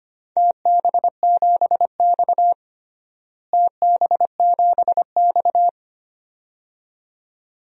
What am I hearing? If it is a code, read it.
Morse code, heard as a keyed tone, about 25 wpm, 700 Hz. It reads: TB7X TB7X
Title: DXCC-100 Call Signs - 4 Characters - Repeated 2x - 25wpm